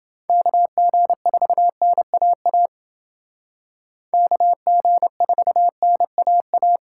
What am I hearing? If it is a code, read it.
Morse code, heard as a keyed tone, about 28 wpm, 700 Hz. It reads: KG4NAA KG4NAA